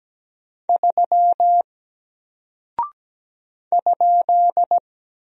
Morse code, heard as a keyed tone, about 17 wpm, 700 Hz. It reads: 3 ?
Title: Single Character (Letter:Number:Punctuation:Prosign) Alt 17wpm